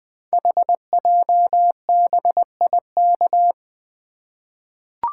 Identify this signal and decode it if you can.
Morse code, heard as a keyed tone, about 21 wpm, 700 Hz. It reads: HJBIK
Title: Code Groups - 5 Letters - 20wpm